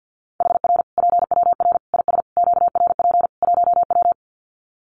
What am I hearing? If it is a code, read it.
Morse code, heard as a keyed tone, about 50 wpm, 700 Hz. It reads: HR PWR IS QRP 1W
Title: Common QSO Phrases 50wpm